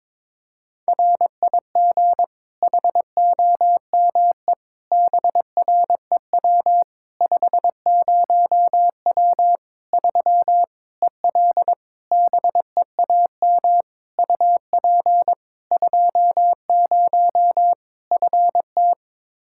Morse code, heard as a keyed tone, about 22 wpm, 700 Hz. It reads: RIG HOME BREW 50W 3 EL BEAM UP 20 FT
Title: Practice QSO - Exchange 2 - Info - 22wpm